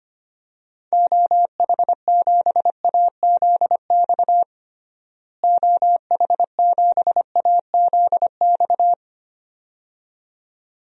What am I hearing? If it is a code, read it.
Morse code, heard as a keyed tone, about 25 wpm, 700 Hz. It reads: OH7AZX OH7AZX
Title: DXCC-100 Call Signs - 6 Characters - Repeated 2x - 25wpm